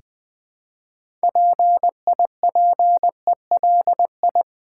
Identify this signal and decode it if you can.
Morse code, heard as a keyed tone, about 20 wpm, 700 Hz. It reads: PIPELI
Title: Build A Word - 3 Letters At A Time - 8-Letter Words - 20wpm